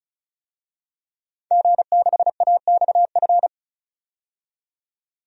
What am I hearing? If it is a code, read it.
Morse code, heard as a keyed tone, about 36 wpm, 700 Hz. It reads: G6AXF
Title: DXCC-100 Call Signs - 5 Characters - 35wpm